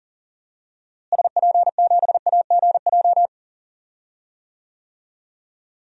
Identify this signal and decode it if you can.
Morse code, heard as a keyed tone, about 40 wpm, 700 Hz. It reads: SP7AGJ
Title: DXCC-100 Call Signs - 6 Characters - Repeated 2x - 40wpm